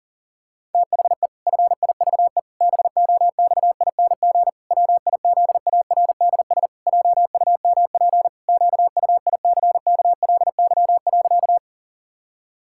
Morse code, heard as a keyed tone, about 37 wpm, 700 Hz. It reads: THE FIVE BOXING WIZARDS JUMP QUICKLY.
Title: Pangram Challenge v2 - 40wpm